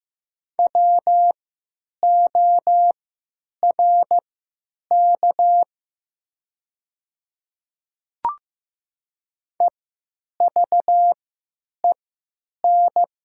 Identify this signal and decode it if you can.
Morse code, heard as a keyed tone, about 15 wpm, spaced as for 9 wpm, 700 Hz. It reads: WORK EVEN
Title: 3x Character Spacing - Top 200 Words - 15wpm